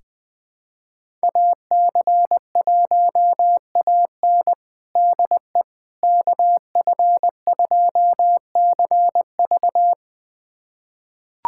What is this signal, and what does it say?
Morse code, heard as a keyed tone, about 21 wpm, 700 Hz. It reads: AC1AN DE KF2CV
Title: USA Callsign DE Callsign - 5 Characters - 20wpm